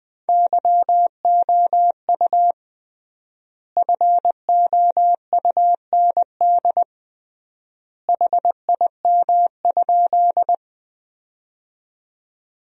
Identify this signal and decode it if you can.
Morse code, heard as a keyed tone, about 21 wpm, 700 Hz. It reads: YOU FOUND HIM?
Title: 3x Word-Spacing - Sentences from Top 300 Words - 20wpm